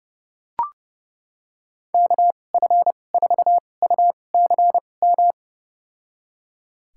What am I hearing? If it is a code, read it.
Morse code, heard as a keyed tone, about 29 wpm, spaced as for 21 wpm, 700 Hz. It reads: KF4UCM
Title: USA Call Signs - 6 Characters - 2x Character Spacing - 30wpm